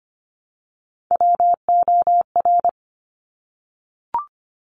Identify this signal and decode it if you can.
Morse code, heard as a keyed tone, about 25 wpm, 700 Hz. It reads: WOR
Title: Top Three Letter Combinations in Words 25wpm